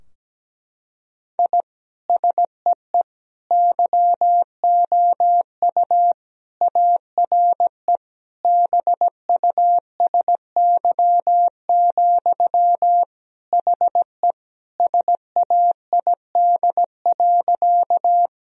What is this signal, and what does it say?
Morse code, heard as a keyed tone, about 17 wpm, 700 Hz. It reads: I SEE YOU ARE BUSY, HE SAID.
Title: Sentences from Top 1000 Words - Review Part 3 - 17wpm